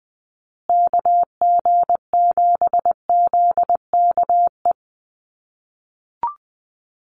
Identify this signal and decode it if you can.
Morse code, heard as a keyed tone, about 20 wpm, 700 Hz. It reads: KG7ZKE
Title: USA Call Signs Repeated 3x 20wpm